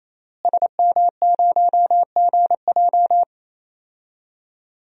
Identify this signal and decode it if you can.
Morse code, heard as a keyed tone, about 28 wpm, 700 Hz. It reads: SM0GJ
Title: DXCC-100 Call Signs - 5 Characters - Repeated 3x - 28wpm